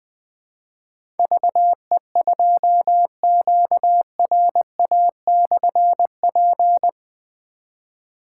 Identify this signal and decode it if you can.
Morse code, heard as a keyed tone, about 20 wpm, 700 Hz. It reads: VE2QRA/P
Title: Difficult Call Signs 20wpm